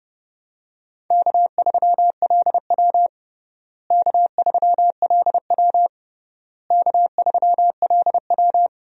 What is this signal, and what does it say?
Morse code, heard as a keyed tone, about 30 wpm, 700 Hz. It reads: K3LW K3LW K3LW
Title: USA Call Signs - 4 Characters - Repeated 3x - 30wpm